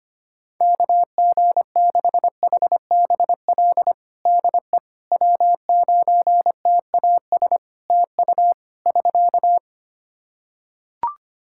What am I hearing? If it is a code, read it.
Morse code, heard as a keyed tone, about 25 wpm, 700 Hz. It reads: KG6HBL DE W9TAS TU <SK>
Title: Practice QSO - Exchange 4 - Ending - 25wpm